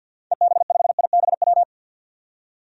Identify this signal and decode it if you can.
Morse code, heard as a keyed tone, about 50 wpm, 700 Hz. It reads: EB5IBW
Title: DXCC-100 Call Signs - 6 Characters - 50wpm